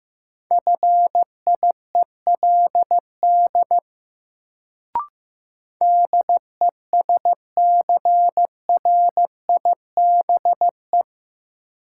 Morse code, heard as a keyed tone, about 16 wpm, 700 Hz. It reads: FIELD DESCRIBE
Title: Sentences from Top 1000 Words - New Word Drill - 15wpm